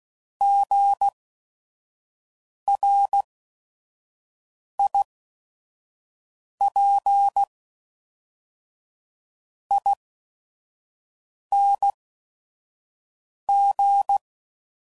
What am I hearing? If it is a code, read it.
Morse code, heard as a keyed tone, about 16 wpm, spaced as for 5 wpm, 800 Hz. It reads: GRIPING
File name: Prefixes / By First Letter / Gs